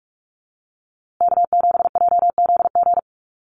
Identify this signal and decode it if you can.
Morse code, heard as a keyed tone, about 44 wpm, 700 Hz. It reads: K7JZG
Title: USA Call Signs Repeated 3x 45wpm